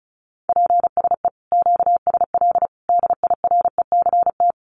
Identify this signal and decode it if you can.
Morse code, heard as a keyed tone, about 34 wpm, 700 Hz. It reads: PSE QSL DIRECT
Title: Common QSO Phrases 35wpm